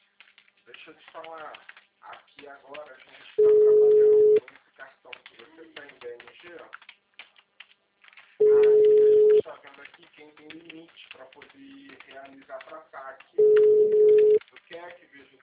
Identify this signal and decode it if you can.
Ringback tone (Europe, 425 Hz), with speech.
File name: out- -2033